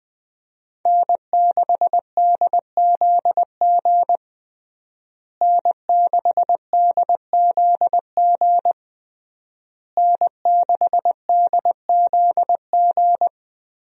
Morse code, heard as a keyed tone, about 20 wpm, 700 Hz. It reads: N6DZG N6DZG N6DZG
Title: USA Call Signs - 5 Characters - Repeated 3x - 20wpm